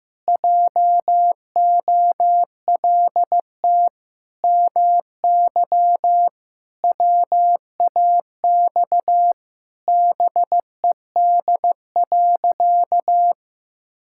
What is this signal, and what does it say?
Morse code, heard as a keyed tone, about 15 wpm, 700 Hz. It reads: JOLT MY WAX BED.
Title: Pangram Challenge v2 - 15wpm